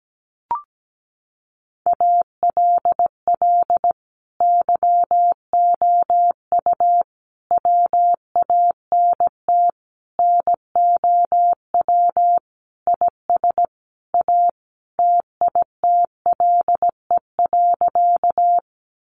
Morse code, heard as a keyed tone, about 17 wpm, 700 Hz. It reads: ALL YOU WANT NOW IS A TITLE.
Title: Sentences from Top 1200 Words 17wpm